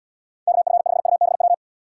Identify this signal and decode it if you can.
Morse code, heard as a keyed tone, about 78 wpm, 700 Hz. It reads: KF5UBW